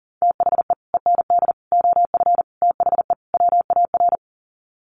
Morse code, heard as a keyed tone, about 40 wpm, 700 Hz. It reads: THE END OF THE WAR
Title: Sets of 5 Words 40wpm